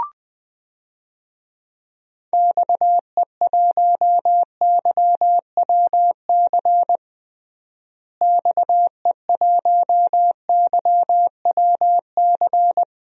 Morse code, heard as a keyed tone, about 20 wpm, 700 Hz. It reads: XE1YWC XE1YWC
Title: DXCC-100 Call Signs - 6 Characters - Repeated 3x - 20wpm